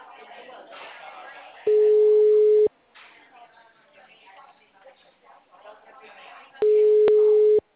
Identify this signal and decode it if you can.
Ringback tone (Europe, 425 Hz), with speech.